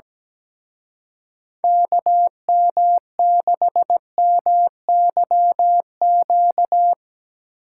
Morse code, heard as a keyed tone, about 20 wpm, 700 Hz. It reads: KM6MYQ
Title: Speed Racing - USA 2x3 Call Signs 17wpm